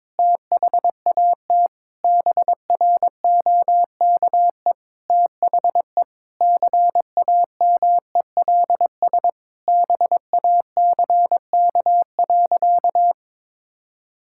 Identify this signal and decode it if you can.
Morse code, heard as a keyed tone, about 22 wpm, 700 Hz. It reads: THAT BROKE THE CAMELS BACK.
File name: practice / 22wpm / Sentences with 8-Word Phrases - 22wpm